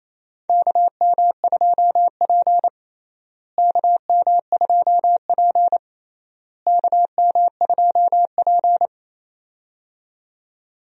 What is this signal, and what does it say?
Morse code, heard as a keyed tone, about 28 wpm, 700 Hz. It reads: KM2P KM2P KM2P
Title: USA Call Signs - 4 Characters - Repeated 3x - 28wpm